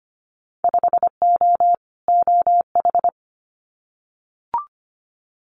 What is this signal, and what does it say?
Morse code, heard as a keyed tone, about 25 wpm, 700 Hz. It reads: 5O OH
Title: Winter Field Day Exchanges Encore 25wpm